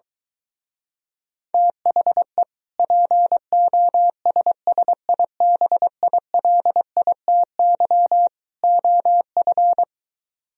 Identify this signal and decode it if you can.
Morse code, heard as a keyed tone, about 24 wpm, 700 Hz. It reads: THE POSSIBILITY OF